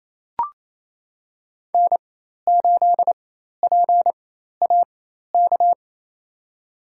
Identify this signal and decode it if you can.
Morse code, heard as a keyed tone, about 28 wpm, spaced as for 13 wpm, 700 Hz. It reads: N8PAK
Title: USA Call Signs - 4x Character Spacing - 28wpm